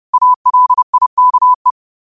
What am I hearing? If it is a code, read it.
Morse code, heard as a keyed tone, about 30 wpm, 1000 Hz. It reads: ALIKE
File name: 30wpm